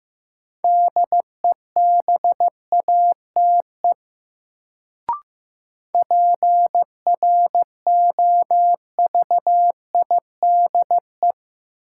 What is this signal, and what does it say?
Morse code, heard as a keyed tone, about 15 wpm, 700 Hz. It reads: DEBATE PROVIDE
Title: Sentences from Top 1100 Words - New Word Drill Alt - 15wpm